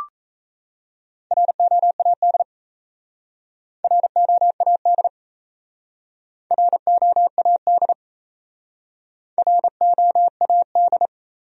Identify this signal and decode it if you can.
Morse code, heard as a keyed tone, about 34 wpm, 700 Hz. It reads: ROAD ROAD ROAD ROAD
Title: Speed-Racing Top 500 Words 28wpm